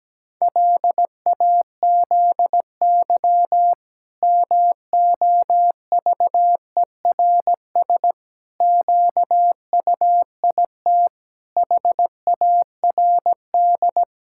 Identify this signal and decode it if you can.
Morse code, heard as a keyed tone, about 17 wpm, 700 Hz. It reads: LAZY MOVERS QUIT HARD
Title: Pangram Challenge v2 - 17wpm